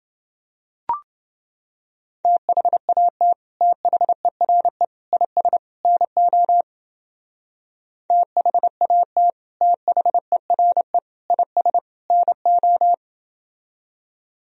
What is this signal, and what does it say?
Morse code, heard as a keyed tone, about 28 wpm, 700 Hz. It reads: THAT THERE IS NO THAT THERE IS NO